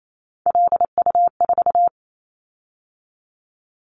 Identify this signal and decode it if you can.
Morse code, heard as a keyed tone, about 28 wpm, 700 Hz. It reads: LU4
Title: Callsign Prefixes for Top 100 DXCC - 28wpm